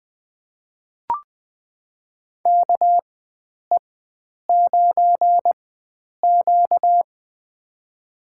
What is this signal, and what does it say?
Morse code, heard as a keyed tone, about 20 wpm, spaced as for 9 wpm, 700 Hz. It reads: KE9Q
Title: USA Call Signs - 4 Characters - 4x Character Spacing - 20wpm